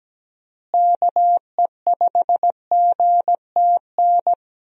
Morse code, heard as a keyed tone, about 17 wpm, 700 Hz. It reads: KE5GTN
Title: USA Call Signs - 6 Characters - Repeated 2x - 17wpm